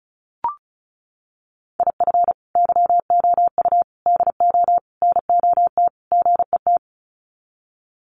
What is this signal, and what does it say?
Morse code, heard as a keyed tone, about 34 wpm, 700 Hz. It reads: IF YOU DO NOT GET
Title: Sets of 5 Words Encore2 35wpm